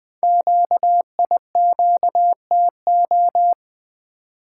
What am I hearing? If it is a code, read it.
Morse code, heard as a keyed tone, about 20 wpm, 700 Hz. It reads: QIQTO
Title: Code Groups - 5 Letters - 20wpm